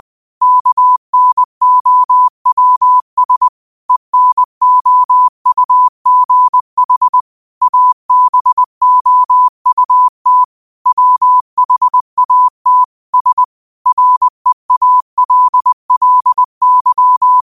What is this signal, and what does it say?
Morse code, heard as a keyed tone, about 20 wpm, 1000 Hz. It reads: KNOWS ENOUGH ABOUT WHAT S REALLY
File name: qotdfiles / 20wpm